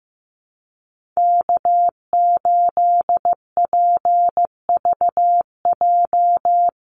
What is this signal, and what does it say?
Morse code, heard as a keyed tone, about 15 wpm, 700 Hz. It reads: K8PVJ
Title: USA Call Signs Repeated 3x 15wpm